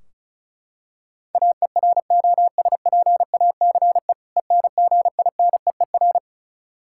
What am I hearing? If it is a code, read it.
Morse code, heard as a keyed tone, about 36 wpm, 700 Hz. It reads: AEROSPACE ENGINEER
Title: K1USN SST QSO Practice 35wpm